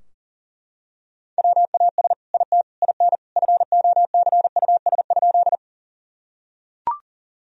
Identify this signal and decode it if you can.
Morse code, heard as a keyed tone, about 40 wpm, 700 Hz. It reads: WAS IT IN FOCUS?
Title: Sentences from Top 1000 Words - Review Part 4 - 40wpm